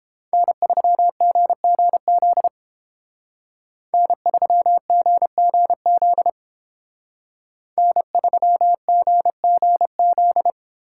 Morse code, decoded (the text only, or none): N3GGZ N3GGZ N3GGZ